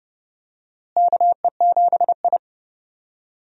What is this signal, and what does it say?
Morse code, heard as a keyed tone, about 30 wpm, 700 Hz. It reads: KE7I
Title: USA Call Signs - 4 Characters - Repeated 2x - 30wpm